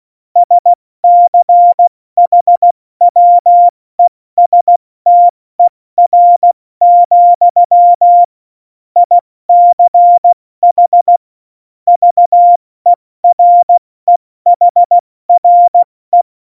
Morse code, heard as a keyed tone, about 16 wpm, 700 Hz. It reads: SCHWESTER, ICH VEREHRE
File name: Morsekurs / Witze / Witz 08. September 2020, 70 F80